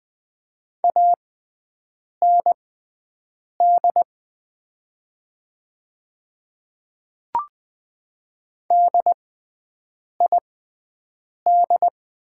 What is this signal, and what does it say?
Morse code, heard as a keyed tone, about 20 wpm, spaced as for 7 wpm, 700 Hz. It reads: AND DID
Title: 6x Character Spacing - Top 100 Words - 20wpm